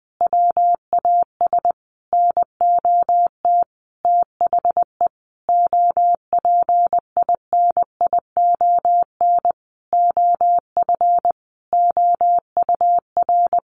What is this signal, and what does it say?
Morse code, heard as a keyed tone, about 19 wpm, 700 Hz. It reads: WAS NOT THE OPINION OF OUR
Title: Sentences from Top 1400 Words 20wpm